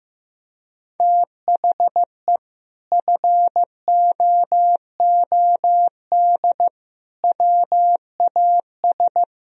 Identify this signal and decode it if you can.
Morse code, heard as a keyed tone, about 15 wpm, 700 Hz. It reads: THE FOOD WAS
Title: Sentences from Top 1000 Words - Review Part 1 - 15wpm